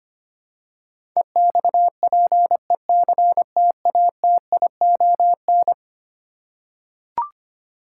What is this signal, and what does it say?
Morse code, heard as a keyed tone, about 25 wpm, 700 Hz. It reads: EXPECTATION
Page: morse-code-ninja-practice-25wpm